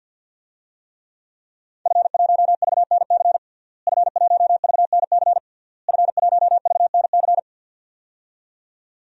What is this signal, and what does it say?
Morse code, heard as a keyed tone, about 49 wpm, 700 Hz. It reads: F1VNC F1VNC F1VNC
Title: DXCC-100 Call Signs - 5 Characters - Repeated 3x - 50wpm